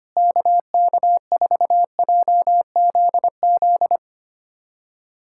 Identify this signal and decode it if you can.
Morse code, heard as a keyed tone, about 25 wpm, 700 Hz. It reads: KK4JZZ
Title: USA Callsign DE Callsign 25wpm